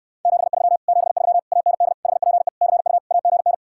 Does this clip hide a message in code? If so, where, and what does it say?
Yes: Morse code from 0.3 to 3.5 s, heard as a keyed tone, about 69 wpm, 700 Hz. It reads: 73 73 AND HPE CU AGN